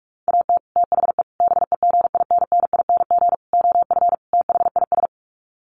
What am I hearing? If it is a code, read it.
Morse code, heard as a keyed tone, about 44 wpm, 700 Hz. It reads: AT THE BEGINNING OF THIS
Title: Sets of 5 Words Encore 45wpm